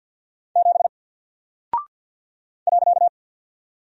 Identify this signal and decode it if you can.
Morse code, heard as a keyed tone, about 50 wpm, 700 Hz. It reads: 7 .
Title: Single Character (Letter:Number:Punctuation:Prosign) 50wpm